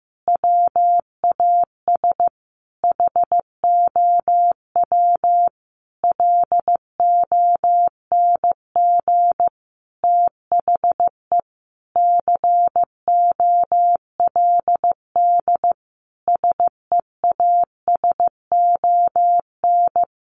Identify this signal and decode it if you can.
Morse code, heard as a keyed tone, about 15 wpm, 700 Hz. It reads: WAS HOW LONG THE COLD SEASON
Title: Sentences from Top 1400 Words 15wpm